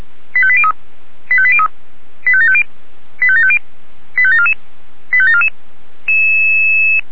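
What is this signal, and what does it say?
Five-tone selective call (ZVEI), heard as quick runs of five tones, tones of about 70 ms. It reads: 85803 85803 85750 85750 85744 85744
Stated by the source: ZVEI five-tone selective call received on an SDR.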